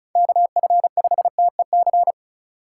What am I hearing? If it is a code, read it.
Morse code, heard as a keyed tone, about 34 wpm, 700 Hz. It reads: KF5TEC
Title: USA Call Signs - 6 Characters - 3x Character Spacing - 35wpm